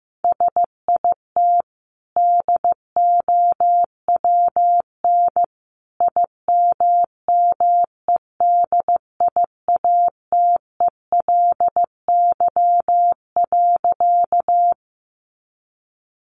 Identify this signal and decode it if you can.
Morse code, heard as a keyed tone, about 14 wpm, 700 Hz. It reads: SIT DOWN IMMEDIATELY.
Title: Sentences from Top 1100 Words 15wpm